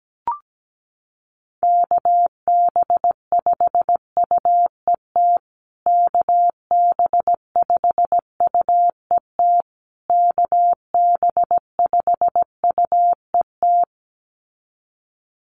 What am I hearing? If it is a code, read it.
Morse code, heard as a keyed tone, about 17 wpm, 700 Hz. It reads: KB5UET KB5UET KB5UET
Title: USA Call Signs Repeated 3x 17wpm